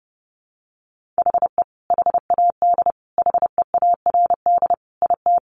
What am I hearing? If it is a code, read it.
Morse code, heard as a keyed tone, about 30 wpm, 700 Hz. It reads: HE HAD HEARD IT
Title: Sentences from Top 1400 Words 30wpm